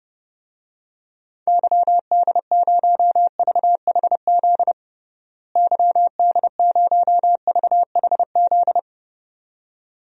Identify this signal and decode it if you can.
Morse code, heard as a keyed tone, about 32 wpm, 700 Hz. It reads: YD0VHZ YD0VHZ
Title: DXCC-100 Call Signs - 6 Characters - Repeated 2x - 30wpm